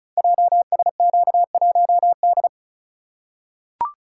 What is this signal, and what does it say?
Morse code, heard as a keyed tone, about 36 wpm, 690 Hz. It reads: JSQ1D
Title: Code Groups - 5 Letter:Numbers - 35wpm